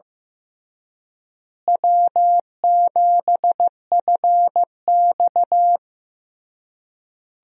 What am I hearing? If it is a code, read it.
Morse code, heard as a keyed tone, about 15 wpm, 700 Hz. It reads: W7FX
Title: USA Call Signs - 4 Characters - Repeated 3x - 15wpm